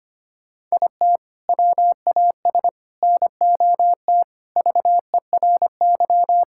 Morse code, heard as a keyed tone, about 25 wpm, 700 Hz. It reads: IT WAS NOT VERY
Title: Sentences from Top 1000 Words - Review Part 5 - 25wpm